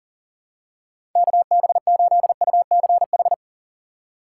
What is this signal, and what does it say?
Morse code, heard as a keyed tone, about 40 wpm, 700 Hz. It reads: KB8UCH
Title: USA Call Signs - 6 Characters - 4x Character Spacing - 40wpm